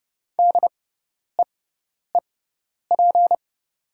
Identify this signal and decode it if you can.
Morse code, heard as a keyed tone, about 29 wpm, spaced as for 10 wpm, 700 Hz. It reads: DEEP
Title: 6x Character Spacing - Top 500 Words - 30wpm